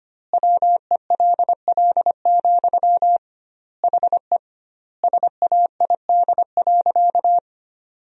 Morse code, heard as a keyed tone, about 25 wpm, 700 Hz. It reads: WELL, HE SAID.
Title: 2x Word-Spacing - Sentences from Top 500 Words - 25wpm